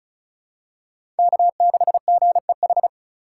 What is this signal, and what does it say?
Morse code, heard as a keyed tone, about 34 wpm, 700 Hz. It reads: K6GEH
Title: USA Call Signs - 5 Characters - 3x Character Spacing - 35wpm